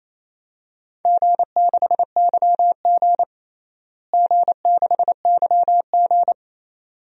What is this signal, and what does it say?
Morse code, heard as a keyed tone, about 28 wpm, 700 Hz. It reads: G6YG G6YG
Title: DXCC-100 Call Signs - 4 Characters - Repeated 2x - 28wpm